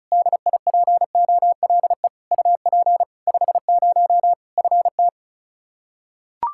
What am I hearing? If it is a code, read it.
Morse code, heard as a keyed tone, about 36 wpm, 700 Hz. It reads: DIPOLE UP 50 FT
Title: Practice QSO - Exchange 2 - Info - 35wpm